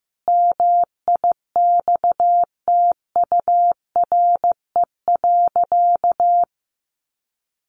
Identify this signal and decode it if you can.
Morse code, heard as a keyed tone, about 14 wpm, 700 Hz. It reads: MIXTURE.
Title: Sentences from Top 1500 Words 15wpm